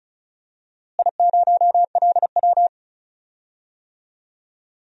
Morse code, heard as a keyed tone, about 36 wpm, 700 Hz. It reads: I0LW
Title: DXCC-100 Call Signs - 4 Characters - Repeated 3x - 35wpm